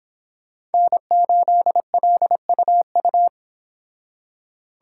Morse code, heard as a keyed tone, about 30 wpm, 700 Hz. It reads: N8LUU